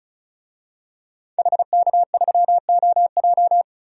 Morse code, heard as a keyed tone, about 34 wpm, 700 Hz. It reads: HK3OJ
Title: DXCC-100 Call Signs - 5 Characters - 35wpm